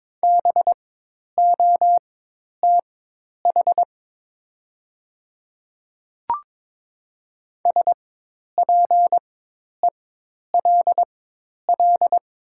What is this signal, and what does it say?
Morse code, heard as a keyed tone, about 22 wpm, spaced as for 10 wpm, 700 Hz. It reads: BOTH SPELL
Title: 4x Character Spacing - Top 300 Words - 22wpm